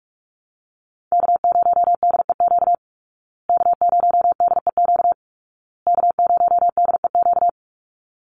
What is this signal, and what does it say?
Morse code, heard as a keyed tone, about 44 wpm, 700 Hz. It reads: K0DEQ K0DEQ K0DEQ
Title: Call Signs Repeated 2x - 45wpm